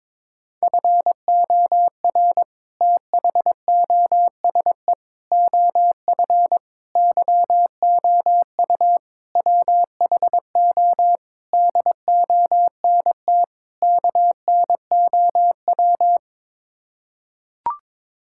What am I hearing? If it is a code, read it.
Morse code, heard as a keyed tone, about 21 wpm, 700 Hz. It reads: FOR THOSE OF YOU WHO DONT KNOW
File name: practice / 22wpm / Sets of 7 Words - 22wpm